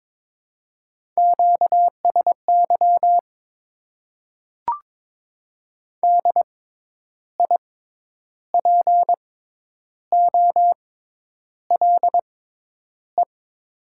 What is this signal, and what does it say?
Morse code, heard as a keyed tone, about 22 wpm, spaced as for 9 wpm, 700 Hz. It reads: QSY DIPOLE
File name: Top 90 QSO Elements - 6x Character Spacing - 22wpm